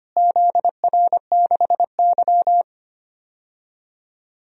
Morse code, heard as a keyed tone, about 25 wpm, 700 Hz. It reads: ZR6Y
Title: DXCC-100 Call Signs - 4 Characters - Repeated 2x - 25wpm